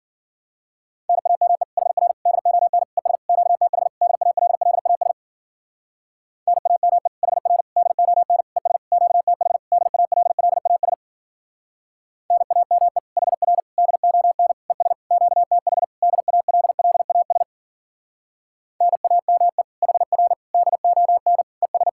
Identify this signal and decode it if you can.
Morse code, heard as a keyed tone, about 50 wpm, 700 Hz. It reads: NAME HR DON ES QTH DALLAS NAME HR DON ES QTH DALLAS NAME HR DON ES QTH DALLAS NAME HR DON ES